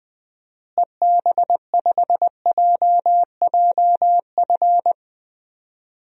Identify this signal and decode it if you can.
Morse code, heard as a keyed tone, about 20 wpm, 700 Hz. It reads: EB5JJF
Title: DXCC-100 Call Signs - 6 Characters - Repeated 2x - 20wpm